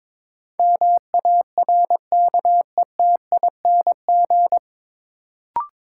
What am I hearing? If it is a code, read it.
Morse code, heard as a keyed tone, about 22 wpm, 700 Hz. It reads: MARKETING
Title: Sentences from Top 1000 Words - Review Part 5 - New Word Drill - 22wpm